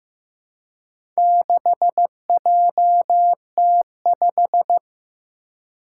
Morse code, heard as a keyed tone, about 15 wpm, 700 Hz. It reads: 6JT5
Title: Four Challenging Letter-Number Combinations 15wpm